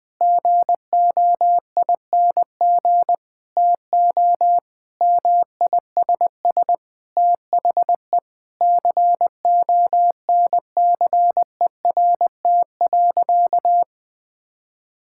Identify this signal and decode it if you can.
Morse code, heard as a keyed tone, about 20 wpm, 700 Hz. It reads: GOING TO MISS THE CONCERT.
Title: Sentences with 8-Word Phrases - 20wpm